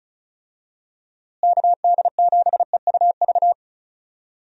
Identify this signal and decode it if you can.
Morse code, heard as a keyed tone, about 34 wpm, 700 Hz. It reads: KD7EUV
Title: USA Call Signs - 6 Characters - Repeated 3x - 35wpm